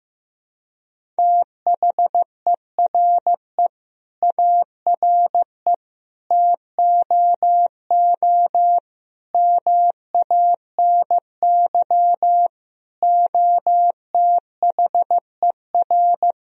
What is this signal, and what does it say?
Morse code, heard as a keyed tone, about 15 wpm, 700 Hz. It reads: THERE ARE TOO MANY OTHER
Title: Sentences from Top 1000 Words - Review Part 3 - 15wpm